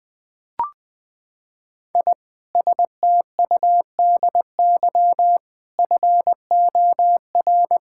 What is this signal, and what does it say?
Morse code, heard as a keyed tone, about 21 wpm, 700 Hz. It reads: I STUDY FOR
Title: Sentences with 7-Word Phrases - 20wpm